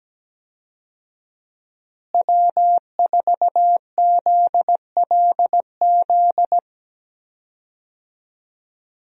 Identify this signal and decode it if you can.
Morse code, heard as a keyed tone, about 17 wpm, 700 Hz. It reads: W4ZLZ
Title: USA Call Signs - 5 Characters - Repeated 3x - 17wpm